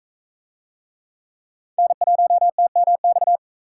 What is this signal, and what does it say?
Morse code, heard as a keyed tone, about 41 wpm, 700 Hz. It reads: N1TMX